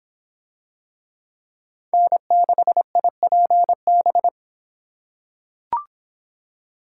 Morse code, heard as a keyed tone, about 25 wpm, 700 Hz. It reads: N6IPB